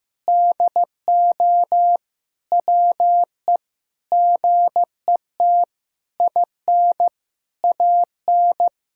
Morse code, heard as a keyed tone, about 15 wpm, 700 Hz. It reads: DO WE GET IN AN
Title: Sentences from Top 1000 Words - Review Part 6 - 15wpm